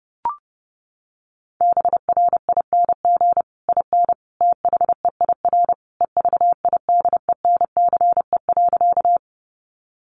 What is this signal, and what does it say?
Morse code, heard as a keyed tone, about 30 wpm, 700 Hz. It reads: BRING IN THEIR EVIDENCE.
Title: Sentences from Top 1300 Words 30wpm